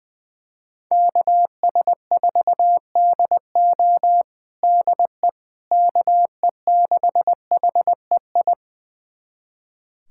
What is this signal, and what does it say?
Morse code, heard as a keyed tone, about 21 wpm, 700 Hz. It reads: KS4DO DE KE6HEI
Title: USA Callsign DE Callsign 20wpm